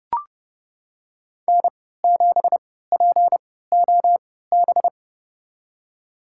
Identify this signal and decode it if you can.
Morse code, heard as a keyed tone, about 30 wpm, spaced as for 17 wpm, 700 Hz. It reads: N7POB
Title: USA Call Signs - 5 Characters - 3x Character Spacing - 30wpm